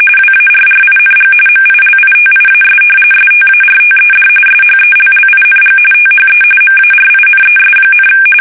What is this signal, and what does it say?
RTTY, about 45.45 baud. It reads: WARRING MILITIAS STARTED SHELLING THE MARINES,
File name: fsk850 / battleshipnj